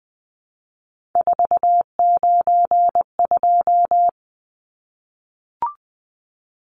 Morse code, heard as a keyed tone, about 20 wpm, 700 Hz. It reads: 492
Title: Numbers - 3-Digit 20wpm